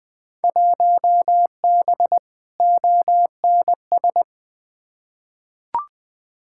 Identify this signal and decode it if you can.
Morse code, heard as a keyed tone, about 20 wpm, 700 Hz. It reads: 1B ONS
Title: Field Day Exchanges Encore - 20wpm